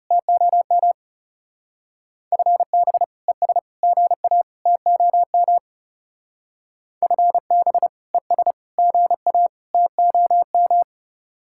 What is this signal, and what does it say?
Morse code, heard as a keyed tone, about 36 wpm, 700 Hz. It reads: TOM FB ES GA TOM FB ES GA TOM
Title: Speed-Racing Common QSO Phrases - 30wpm